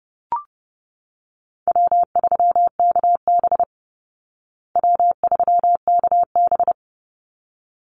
Morse code, heard as a keyed tone, about 30 wpm, 700 Hz. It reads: W3KB W3KB
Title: Easy Call Signs Repeated 2x - 30wpm